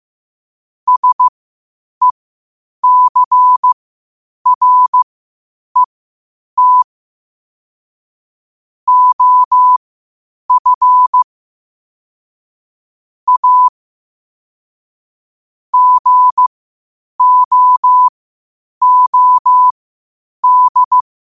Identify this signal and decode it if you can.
Morse code, heard as a keyed tone, about 15 wpm, spaced as for 8 wpm, 1000 Hz. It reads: SECRET OF A GOOD
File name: qotdfiles / 8wpm